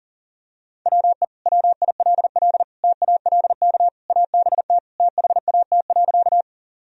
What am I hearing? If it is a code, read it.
Morse code, heard as a keyed tone, about 40 wpm, 700 Hz. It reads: WE WILL TALK ABT THAT.
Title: Sentences from Top 400 Words and Abbreviations 40wpm